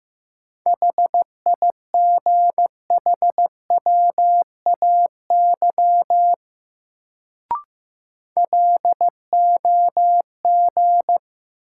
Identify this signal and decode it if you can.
Morse code, heard as a keyed tone, about 15 wpm, 700 Hz. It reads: HIGHWAY LOG
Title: Sentences from Top 1400 Words - New Word Drill Alt - 15wpm